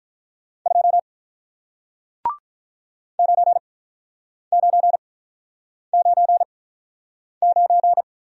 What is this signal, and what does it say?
Morse code, heard as a keyed tone, about 48 wpm, spaced as for 9 wpm, 690 Hz. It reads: 2 9999